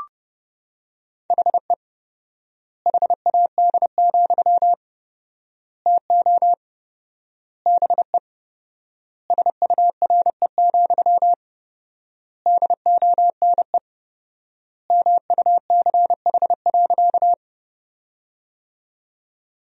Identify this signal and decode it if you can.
Morse code, heard as a keyed tone, about 29 wpm, 700 Hz. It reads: HE HAD, TO BE SURE, DONE MUCH.
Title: 4x Word-Spacing - Sentences from Top 500 Words - 30wpm